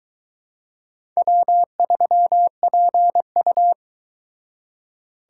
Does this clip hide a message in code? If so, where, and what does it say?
Yes: Morse code from 1.2 to 3.7 s, heard as a keyed tone, about 22 wpm, 700 Hz. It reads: W3PU